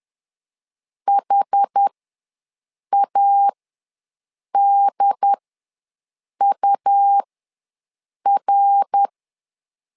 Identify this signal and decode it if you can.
Morse code, heard as a keyed tone, about 10 wpm, spaced as for 6 wpm, 800 Hz. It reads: HADUR
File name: LEZIONI / 3 MWRO / 3 MWR ws 10wpm 2151